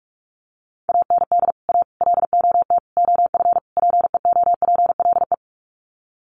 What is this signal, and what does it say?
Morse code, heard as a keyed tone, about 44 wpm, 700 Hz. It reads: AND A LOT OF PEOPLE
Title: Sets of 5 Words 45wpm